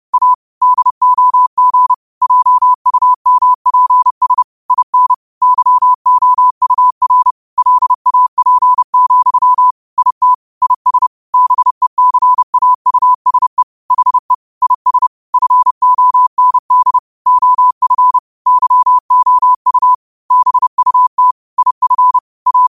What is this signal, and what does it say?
Morse code, heard as a keyed tone, about 30 wpm, 1000 Hz. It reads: A DOG JUMPS IN YOUR LAP, IT IS BECAUSE HE IS FOND OF YOU BUT IF A